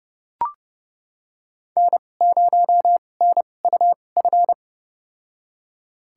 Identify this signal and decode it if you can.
Morse code, heard as a keyed tone, about 32 wpm, spaced as for 22 wpm, 700 Hz. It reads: N0NUF
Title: USA Call Signs - 5 Characters - 2x Character Spacing - 30wpm